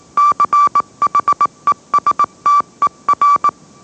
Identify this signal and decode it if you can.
Morse code, heard as a keyed tone, about 18 wpm, 1160 Hz. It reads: CHESTER